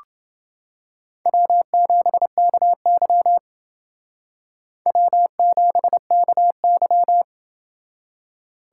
Morse code, heard as a keyed tone, about 28 wpm, 700 Hz. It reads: W7KY W7KY